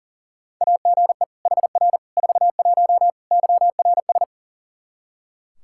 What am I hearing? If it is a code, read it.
Morse code, heard as a keyed tone, about 40 wpm, 690 Hz. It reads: AGE HR 41 YRS
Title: Practice QSO - Exchange 3 - More Info - 40wpm